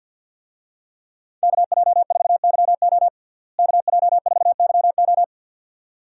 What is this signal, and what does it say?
Morse code, heard as a keyed tone, about 49 wpm, 700 Hz. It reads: KJ4YO KJ4YO
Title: USA Call Signs - 5 Characters - Repeated 2x - 50wpm